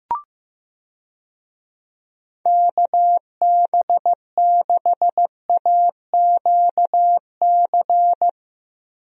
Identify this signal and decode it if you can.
Morse code, heard as a keyed tone, about 15 wpm, 700 Hz. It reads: KB6AQC
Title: USA Call Signs - 6 Characters - 15wpm